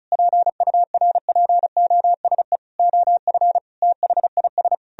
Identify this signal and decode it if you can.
Morse code, heard as a keyed tone, about 45 wpm, 700 Hz. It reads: PURPOSE OF THIS